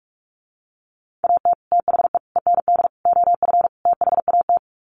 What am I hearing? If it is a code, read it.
Morse code, heard as a keyed tone, about 44 wpm, 700 Hz. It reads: AT THE END OF THAT
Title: Sets of 5 Words Encore2 45wpm